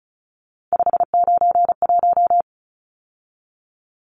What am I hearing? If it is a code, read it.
Morse code, heard as a keyed tone, about 34 wpm, 700 Hz. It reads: 591
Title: Numbers - 3-Digit 35wpm